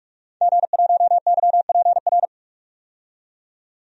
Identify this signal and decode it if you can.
Morse code, heard as a keyed tone, about 45 wpm, 700 Hz. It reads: G1YPR
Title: DXCC-100 Call Signs - 5 Characters - 45wpm